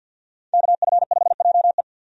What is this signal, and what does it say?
Morse code, heard as a keyed tone, about 49 wpm, 700 Hz. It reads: KF5JE
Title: USA Call Signs - 5 Characters - 50wpm